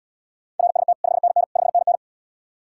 Morse code, heard as a keyed tone, about 77 wpm, 700 Hz. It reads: RST 5NN 5NN